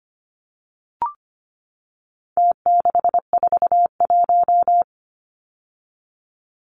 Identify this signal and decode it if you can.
Morse code, heard as a keyed tone, about 25 wpm, 700 Hz. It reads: T641
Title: Numbers - Contest Style - 4-Digit 25wpm